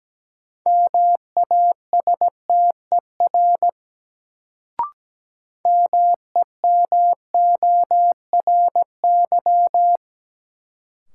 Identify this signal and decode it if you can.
Morse code, heard as a keyed tone, about 17 wpm, 700 Hz. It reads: MASTER MEMORY
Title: Sentences from Top 1300 Words - New Word Drill - 17wpm